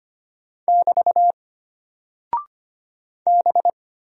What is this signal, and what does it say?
Morse code, heard as a keyed tone, about 25 wpm, 700 Hz. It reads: <BT> B
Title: Single Character (Letter:Number:Punctuation:Prosign) Alt 25wpm